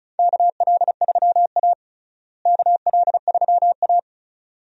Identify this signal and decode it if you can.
Morse code, heard as a keyed tone, about 36 wpm, 700 Hz. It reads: KL3A KL3A
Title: USA Call Signs - 4 Characters - Repeated 2x - 35wpm